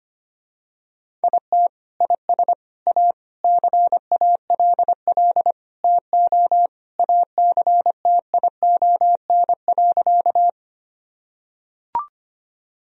Morse code, heard as a keyed tone, about 25 wpm, 700 Hz. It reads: IT IS A CALL TO ACTION.
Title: Sentences from Top 1000 Words - Review Part 4 - 25wpm